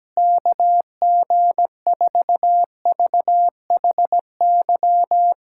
Morse code, heard as a keyed tone, about 17 wpm, 700 Hz. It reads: KG4VHY
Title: USA Callsign DE Callsign 17wpm